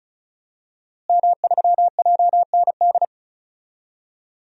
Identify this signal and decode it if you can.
Morse code, heard as a keyed tone, about 36 wpm, 700 Hz. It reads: M3JND
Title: DXCC-100 Call Signs - 5 Characters - Repeated 3x - 35wpm